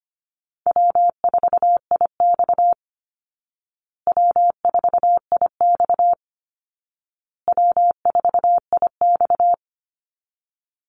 Morse code, heard as a keyed tone, about 25 wpm, 700 Hz. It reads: W4IX W4IX W4IX
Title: Call Signs Repeated 2x - 25wpm